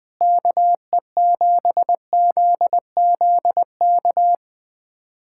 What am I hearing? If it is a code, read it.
Morse code, heard as a keyed tone, about 21 wpm, 700 Hz. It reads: KE7ZZK
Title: USA Callsign DE Callsign - 6 Characters - 20wpm